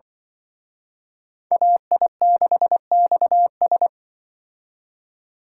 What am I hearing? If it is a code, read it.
Morse code, heard as a keyed tone, about 25 wpm, 700 Hz. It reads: AI6XS
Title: Speed Racing - USA 2x2 Call Signs 20wpm